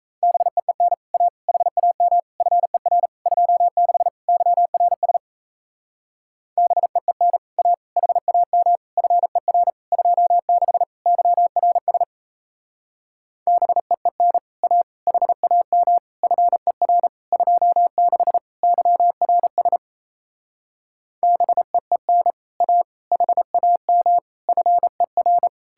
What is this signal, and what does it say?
Morse code, heard as a keyed tone, about 38 wpm, 700 Hz. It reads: BEEN A HAM FER 26 YRS BEEN A HAM FER 26 YRS BEEN A HAM FER 26 YRS BEEN A HAM FER